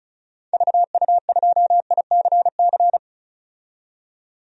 Morse code, heard as a keyed tone, about 36 wpm, 700 Hz. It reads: VU2ICC
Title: DXCC-100 Call Signs - 6 Characters - 35wpm